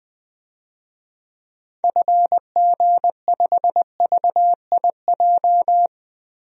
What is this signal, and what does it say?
Morse code, heard as a keyed tone, about 20 wpm, 700 Hz. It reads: FG5VIJ
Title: DXCC-100 Call Signs - 6 Characters - Repeated 2x - 20wpm